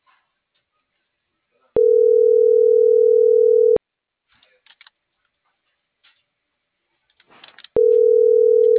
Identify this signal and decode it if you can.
Ringback tone (North America), 440 + 480 Hz.